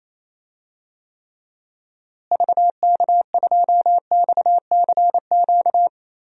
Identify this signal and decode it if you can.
Morse code, heard as a keyed tone, about 28 wpm, 700 Hz. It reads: VK2XCQ
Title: DXCC-100 Call Signs - 6 Characters - Repeated 3x - 28wpm